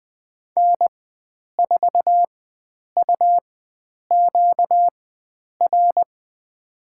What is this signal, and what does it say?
Morse code, heard as a keyed tone, about 20 wpm, spaced as for 9 wpm, 700 Hz. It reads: N4UQR
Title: USA Call Signs - 5 Characters - 4x Character Spacing - 20wpm